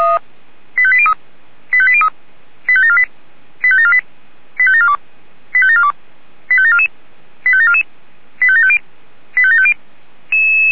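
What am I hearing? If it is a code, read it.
Five-tone selective call (ZVEI), heard as quick runs of five tones, tones of about 70 ms. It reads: 85803 85803 85748 85748 85742 85742 85744 85744 85750 85750
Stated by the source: ZVEI five-tone selective call received by SDR.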